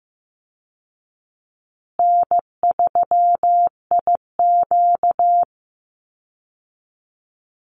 Morse code, heard as a keyed tone, about 15 wpm, 700 Hz. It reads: N3IQ
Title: Easy Call Signs Repeated 3x - 15wpm